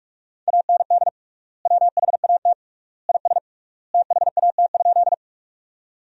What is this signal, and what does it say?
Morse code, heard as a keyed tone, about 46 wpm, 700 Hz. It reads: AND WHAT IS THAT?
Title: 3x Word-Spacing - Sentences from Top 100 Words - 45wpm